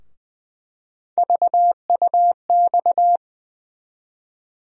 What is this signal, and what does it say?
Morse code, heard as a keyed tone, about 20 wpm, 700 Hz. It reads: VUX
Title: USA Call Sign Postfixes - 3 Letters 20wpm